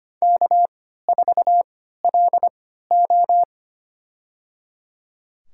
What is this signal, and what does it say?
Morse code, heard as a keyed tone, about 25 wpm, spaced as for 14 wpm, 700 Hz. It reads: K4LO
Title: USA Call Signs - 3x Character Spacing - 25wpm